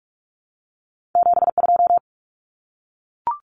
Morse code, heard as a keyed tone, about 46 wpm, 700 Hz. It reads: 72
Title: Numbers - 2-Digit 45wpm